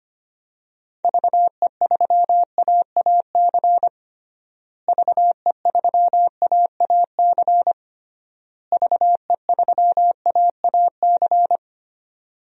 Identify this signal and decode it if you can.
Morse code, heard as a keyed tone, about 25 wpm, 700 Hz. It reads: VE3AAC VE3AAC VE3AAC
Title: DXCC-100 Call Signs - 6 Characters - Repeated 3x - 25wpm